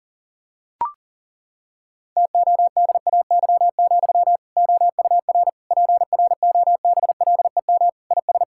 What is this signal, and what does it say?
Morse code, heard as a keyed tone, about 40 wpm, 700 Hz. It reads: TODAY, OUR PROBLEM IS
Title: Sentences from Top 1600 Words 40wpm